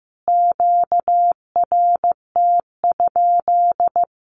Morse code, heard as a keyed tone, about 15 wpm, 700 Hz. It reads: QRT?
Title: Q-Codes Repeated 3x 15wpm